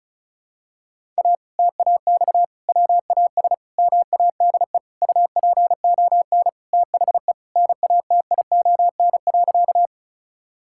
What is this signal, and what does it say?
Morse code, heard as a keyed tone, about 36 wpm, 700 Hz. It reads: A TAX WAS MADE UPON THE NATION.
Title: Sentences from Top 1000 Words - Review Part 6 - 35wpm